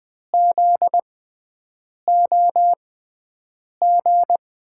Morse code, heard as a keyed tone, about 20 wpm, spaced as for 7 wpm, 700 Hz. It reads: ZOG
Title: USA Call Signs - 6 Characters - 6x Character Spacing - 20wpm